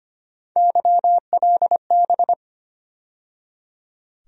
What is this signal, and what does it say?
Morse code, heard as a keyed tone, about 25 wpm, 700 Hz. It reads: YLB
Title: USA Call Sign Postfixes - 3 Letters 25wpm